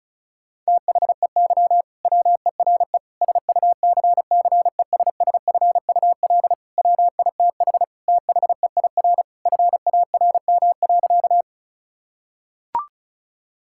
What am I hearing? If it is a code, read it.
Morse code, heard as a keyed tone, about 36 wpm, 700 Hz. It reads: THEY WERE SUCCESSFUL WITH THEIR FARM.
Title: Sentences from Top 1000 Words - Review Part 6 - 35wpm